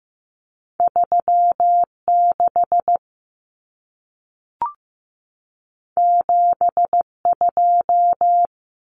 Morse code, heard as a keyed tone, about 15 wpm, 700 Hz. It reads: 36 72
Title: Common Cut Numbers - 2-Digit 15wpm